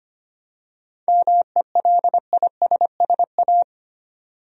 Morse code, heard as a keyed tone, about 25 wpm, 700 Hz. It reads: MELISSA
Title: USA Common Names - 25wpm